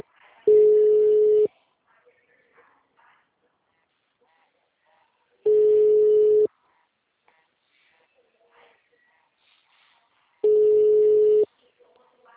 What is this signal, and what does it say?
Ringback tone (Europe, 425 Hz), with speech.